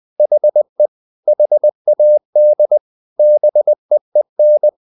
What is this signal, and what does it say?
Morse code, heard as a keyed tone, about 20 wpm, 600 Hz. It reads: HE HAD BEEN
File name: the old man and the sea part 2 20wpm